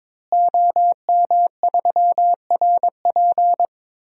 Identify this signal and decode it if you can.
Morse code, heard as a keyed tone, about 21 wpm, 700 Hz. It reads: OM3RP
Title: DXCC-100 Call Signs - 5 Characters - Repeated 3x - 22wpm